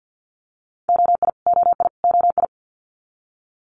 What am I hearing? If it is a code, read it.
Morse code, heard as a keyed tone, about 50 wpm, 700 Hz. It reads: OI OI OI
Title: Multi-letter Phonemes Repeated 3x 50wpm